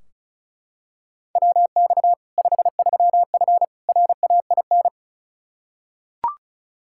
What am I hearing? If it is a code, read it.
Morse code, heard as a keyed tone, about 36 wpm, 700 Hz. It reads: WX 53F RAIN
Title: Practice QSO - Exchange 2 - Info - 35wpm